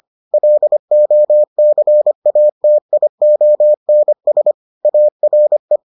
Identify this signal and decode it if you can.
Morse code, heard as a keyed tone, about 25 wpm, 600 Hz. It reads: LOCATIONS ARE